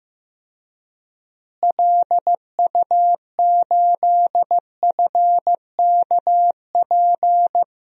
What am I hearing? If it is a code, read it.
Morse code, heard as a keyed tone, about 15 wpm, 700 Hz. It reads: LU8FKP
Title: DXCC-100 Call Signs - 6 Characters - Repeated 3x - 15wpm